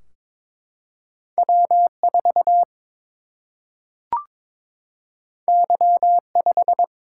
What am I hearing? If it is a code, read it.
Morse code, heard as a keyed tone, about 21 wpm, 700 Hz. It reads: W 4 Y 5
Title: Single-Letter Single-Number 22wpm